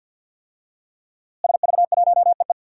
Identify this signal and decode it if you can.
Morse code, heard as a keyed tone, about 51 wpm, 700 Hz. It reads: SV1EE
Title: DXCC-100 Call Signs - 5 Characters - Repeated 2x - 50wpm